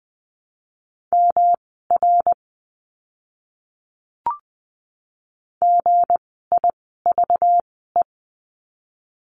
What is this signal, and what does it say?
Morse code, heard as a keyed tone, about 20 wpm, spaced as for 14 wpm, 700 Hz. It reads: MR GIVE
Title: 237 Lesson 38f - Letter q - Words 20wpm